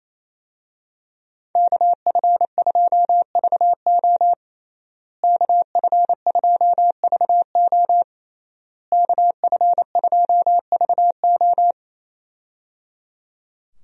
Morse code, heard as a keyed tone, about 28 wpm, 700 Hz. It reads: KF2VO KF2VO KF2VO
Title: USA Call Signs - 5 Characters - Repeated 3x - 28wpm